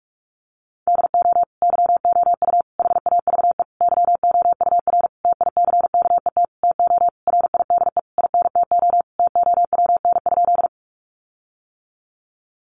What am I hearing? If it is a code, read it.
Morse code, heard as a keyed tone, about 44 wpm, 700 Hz. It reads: DO YOU HAVE YOUR TICKET TO RIDE INTO TOWN?
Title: Sentences from Top 1400 Words 45wpm